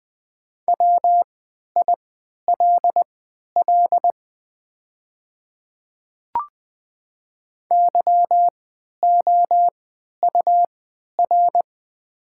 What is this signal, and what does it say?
Morse code, heard as a keyed tone, about 20 wpm, spaced as for 11 wpm, 700 Hz. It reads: WILL YOUR
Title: 3x Character Spacing - Top 200 Words - 20wpm